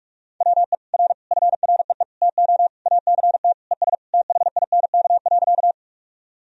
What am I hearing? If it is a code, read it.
Morse code, heard as a keyed tone, about 46 wpm, 700 Hz. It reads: WE R FREE TO ACT ES THINK.
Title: Sentences from Top 500 Words and Abbreviations 45wpm